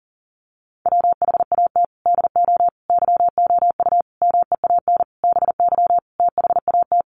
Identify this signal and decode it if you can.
Morse code, heard as a keyed tone, about 40 wpm, 700 Hz. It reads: WHAT DO YOU MEAN BY THAT
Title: Sentences from Top 1200 Words 40wpm